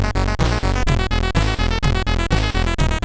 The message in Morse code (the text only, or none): none